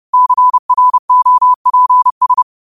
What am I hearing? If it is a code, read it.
Morse code, heard as a keyed tone, about 30 wpm, 1000 Hz. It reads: CROPS
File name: qotdfiles / 30wpm